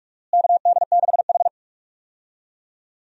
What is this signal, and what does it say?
Morse code, heard as a keyed tone, about 46 wpm, 700 Hz. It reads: KD6H
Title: USA Call Signs - 4 Characters - Repeated 3x - 45wpm